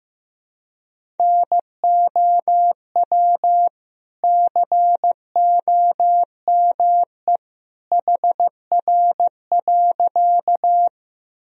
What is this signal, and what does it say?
Morse code, heard as a keyed tone, about 15 wpm, 700 Hz. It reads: NOW COME HR.
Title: Sentences from Top 200 Words and Abbreviations 15wpm